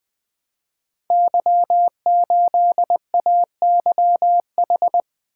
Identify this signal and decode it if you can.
Morse code, heard as a keyed tone, about 20 wpm, 700 Hz. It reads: Y8AYH
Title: Code Groups - 5 Letter:Numbers - 20wpm